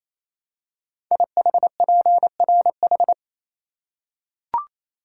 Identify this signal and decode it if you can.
Morse code, heard as a keyed tone, about 27 wpm, 700 Hz. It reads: IHPRH
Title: Code Groups - 5 Letters - 28wpm